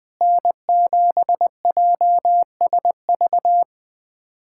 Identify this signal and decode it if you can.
Morse code, heard as a keyed tone, about 20 wpm, 700 Hz. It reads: N7JSV
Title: USA Callsign DE Callsign - 5 Characters - 20wpm